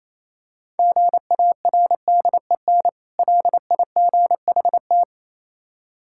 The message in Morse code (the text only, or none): GARDEN LIGHT